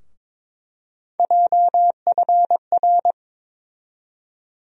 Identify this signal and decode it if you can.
Morse code, heard as a keyed tone, about 22 wpm, 700 Hz. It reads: JFR
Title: USA Call Sign Postfixes - 3 Letters 22wpm